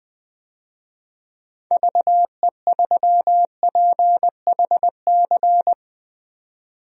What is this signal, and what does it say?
Morse code, heard as a keyed tone, about 20 wpm, 700 Hz. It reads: VE3PHC
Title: DXCC-100 Call Signs - 6 Characters - 20wpm